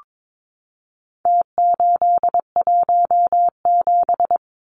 Morse code, heard as a keyed tone, about 22 wpm, 700 Hz. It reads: T817
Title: Numbers - Contest Style - 4-Digit 22wpm